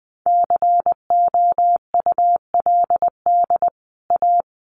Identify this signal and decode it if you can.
Morse code, heard as a keyed tone, about 20 wpm, 700 Hz. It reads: COULD A
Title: Sentences from Top 1200 Words 20wpm